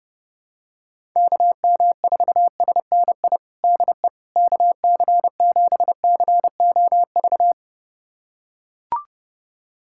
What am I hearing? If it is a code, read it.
Morse code, heard as a keyed tone, about 30 wpm, 700 Hz. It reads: KM4SNI DE KC7COV
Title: USA Callsign DE Callsign 30wpm